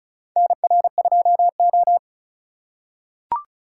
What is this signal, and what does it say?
Morse code, heard as a keyed tone, about 36 wpm, 700 Hz. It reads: NR2O
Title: USA Call Signs - 4 Characters - 2x Character Spacing - 35wpm